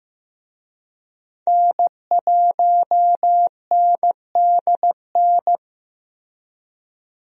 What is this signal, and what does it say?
Morse code, heard as a keyed tone, about 15 wpm, 700 Hz. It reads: N1NDN
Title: USA Call Signs - 5 Characters - Repeated 2x - 15wpm